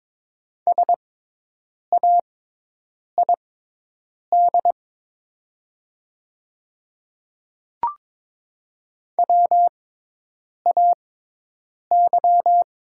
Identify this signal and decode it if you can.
Morse code, heard as a keyed tone, about 23 wpm, spaced as for 8 wpm, 700 Hz. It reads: SAID WAY
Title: 6x Character Spacing - Top 100 Words - 22wpm